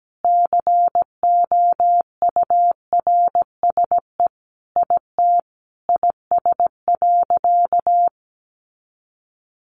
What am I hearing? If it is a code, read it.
Morse code, heard as a keyed tone, about 17 wpm, 700 Hz. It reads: COURSE IT IS.
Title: Sentences from Top 500 Words 17wpm